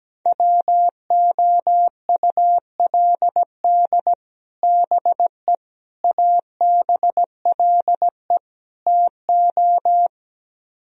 Morse code, heard as a keyed tone, about 17 wpm, 700 Hz. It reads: WOULD BE ABLE TO
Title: Speed-Racing - Sets of 4 Words 17wpm